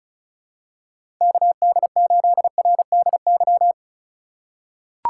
Morse code, heard as a keyed tone, about 36 wpm, 700 Hz. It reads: KD8RDY
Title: USA Call Signs - 6 Characters - 3x Character Spacing - 35wpm